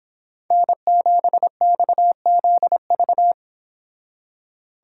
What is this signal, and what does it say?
Morse code, heard as a keyed tone, about 26 wpm, 700 Hz. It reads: N7XZV